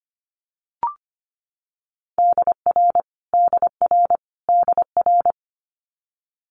Morse code, heard as a keyed tone, about 25 wpm, 700 Hz. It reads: DR DR DR
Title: Multi-letter Phonemes Repeated 3x 25wpm